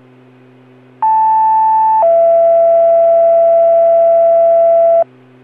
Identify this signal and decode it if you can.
Two-tone page, heard as a one-second tone, then a three-second tone, about 880 then 670 Hz. It is heard from 1.0 to 5.0 s.